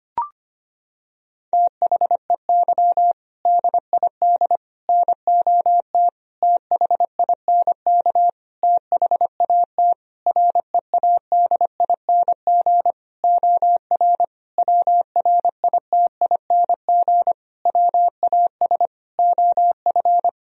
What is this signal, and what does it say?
Morse code, heard as a keyed tone, about 25 wpm, 700 Hz. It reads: THEY DID NOT THINK THAT READING OR WRITING WAS OF
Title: Sentences from Top 1000 Words - Review Part 3 - 25wpm